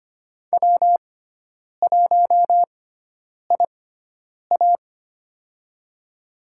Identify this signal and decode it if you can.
Morse code, heard as a keyed tone, about 25 wpm, spaced as for 9 wpm, 700 Hz. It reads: W1IA
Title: USA Call Signs - 6x Character Spacing - 25wpm